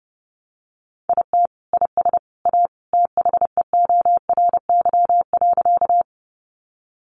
Morse code, heard as a keyed tone, about 30 wpm, 700 Hz. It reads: IT IS A THEORY.
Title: Sentences from Top 600 Words 30wpm